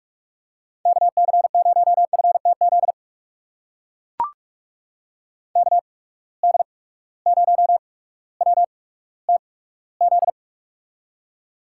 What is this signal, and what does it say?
Morse code, heard as a keyed tone, about 42 wpm, spaced as for 19 wpm, 700 Hz. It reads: KC0FTZ KD0WTZ
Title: USA Call Signs - 6 Characters - 8x Character Spacing - 45wpm